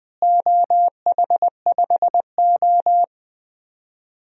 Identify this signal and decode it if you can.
Morse code, heard as a keyed tone, about 21 wpm, 700 Hz. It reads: OH5O
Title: DXCC-100 Call Signs - 4 Characters - Repeated 2x - 20wpm